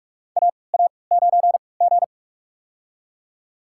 Morse code, heard as a keyed tone, about 43 wpm, spaced as for 25 wpm, 690 Hz. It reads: AA9G
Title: USA Call Signs - 4 Characters - 3x Character Spacing - 45wpm